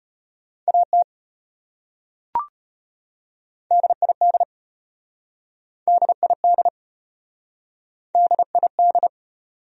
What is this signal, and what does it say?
Morse code, heard as a keyed tone, about 33 wpm, 700 Hz. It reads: AT DID DID DID